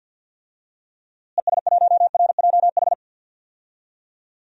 Morse code, heard as a keyed tone, about 46 wpm, 700 Hz. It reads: ES1RJH
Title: DXCC-100 Call Signs - 6 Characters - Repeated 2x - 50wpm